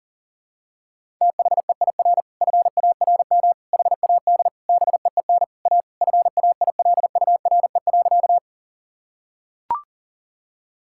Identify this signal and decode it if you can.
Morse code, heard as a keyed tone, about 40 wpm, 700 Hz. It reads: THEIR FARM HAD BEEN A FAILURE.
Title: Sentences from Top 1000 Words - Review Part 6 - 40wpm